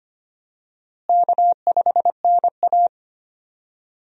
Morse code, heard as a keyed tone, about 25 wpm, 700 Hz. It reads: K5NA
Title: USA Call Signs - 4 Characters - Repeated 3x - 25wpm